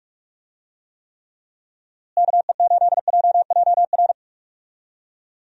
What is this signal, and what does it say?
Morse code, heard as a keyed tone, about 46 wpm, 700 Hz. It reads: KE8JJR
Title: USA Call Signs - 6 Characters - Repeated 3x - 45wpm